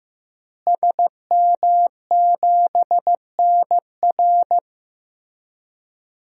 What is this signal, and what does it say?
Morse code, heard as a keyed tone, about 15 wpm, 700 Hz. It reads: SM7NR
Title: DXCC-100 Call Signs - 5 Characters - Repeated 3x - 15wpm